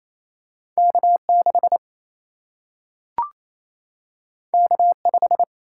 Morse code, heard as a keyed tone, about 28 wpm, 700 Hz. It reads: K6 K5
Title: Single-Letter Single-Number 28wpm